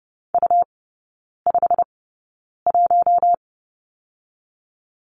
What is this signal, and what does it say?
Morse code, heard as a keyed tone, about 30 wpm, spaced as for 9 wpm, 700 Hz. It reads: U51
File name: Lesson 17e - Letter u - Three Characters 30wpm